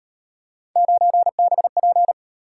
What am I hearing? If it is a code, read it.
Morse code, heard as a keyed tone, about 37 wpm, 700 Hz. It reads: 9BP